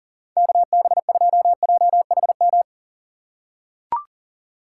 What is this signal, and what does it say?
Morse code, heard as a keyed tone, about 40 wpm, 700 Hz. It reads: KB2JHM
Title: USA Call Signs - 6 Characters - 2x Character Spacing - 40wpm